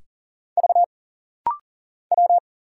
Morse code, heard as a keyed tone, about 40 wpm, 700 Hz. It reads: V W
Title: Single Letters - Rapid-Fire 40wpm